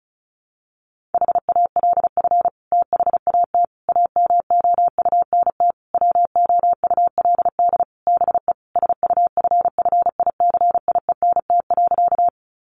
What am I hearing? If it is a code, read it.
Morse code, heard as a keyed tone, about 34 wpm, 700 Hz. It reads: HALF THAT AMOUNT WOULD BE SUFFICIENT.
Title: Sentences from Top 1400 Words 35wpm